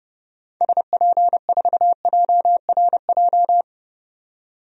Morse code, heard as a keyed tone, about 31 wpm, 700 Hz. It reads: SP4JRJ
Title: DXCC-100 Call Signs - 6 Characters - 30wpm